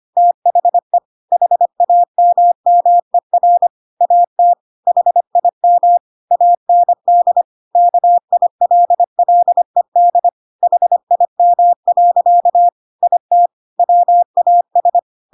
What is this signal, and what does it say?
Morse code, heard as a keyed tone, about 25 wpm, 700 Hz. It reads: THE HAMMER AT HIM AND KILLED HIM. IT WAS
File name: Pinocchio / Pinocchio ch7 25 25 ik1hge